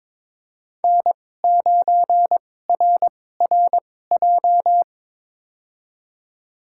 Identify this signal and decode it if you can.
Morse code, heard as a keyed tone, about 22 wpm, spaced as for 16 wpm, 700 Hz. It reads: N9RRJ
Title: USA Call Signs - 5 Characters - 2x Character Spacing - 22wpm